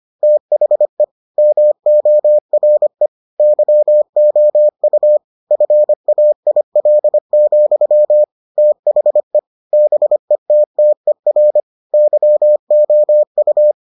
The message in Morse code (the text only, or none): THE MORE YOU FAIL, THE BETTER YOU